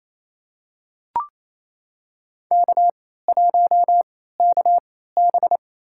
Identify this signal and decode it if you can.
Morse code, heard as a keyed tone, about 28 wpm, spaced as for 16 wpm, 700 Hz. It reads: K1KB
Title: USA Call Signs - 3x Character Spacing - 28wpm